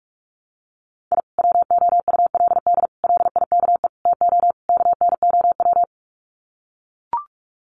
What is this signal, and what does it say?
Morse code, heard as a keyed tone, about 44 wpm, 700 Hz. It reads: I WOULD LIKE TO KNOW
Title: Sets of 5 Words Encore 45wpm